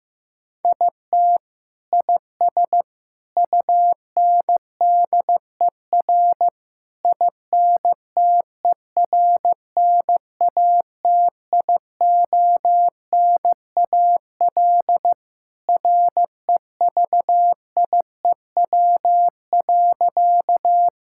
Morse code, heard as a keyed tone, about 15 wpm, 700 Hz. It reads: IT IS UNDER INTERNATIONAL REVIEW.
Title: Sentences from Top 1000 Words - Review Part 6 - 15wpm